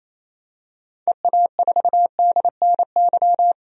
Morse code, heard as a keyed tone, about 28 wpm, 700 Hz. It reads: EA4DNY
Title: DXCC-100 Call Signs - 6 Characters - 28wpm